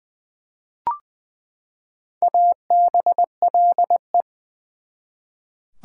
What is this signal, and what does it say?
Morse code, heard as a keyed tone, about 20 wpm, 700 Hz. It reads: ABLE
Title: Prefixes and Suffixes 20wpm